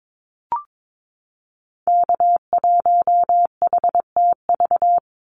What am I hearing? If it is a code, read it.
Morse code, heard as a keyed tone, about 22 wpm, 700 Hz. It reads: K1HTV
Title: Call Signs - 22wpm